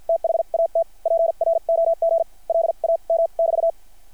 Morse code, heard as a keyed tone, about 50 wpm, spaced as for 39 wpm, 660 Hz. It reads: THAT PROGLAM<BT>
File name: archived / CRSnet